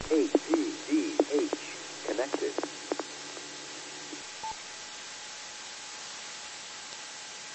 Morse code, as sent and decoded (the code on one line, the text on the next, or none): none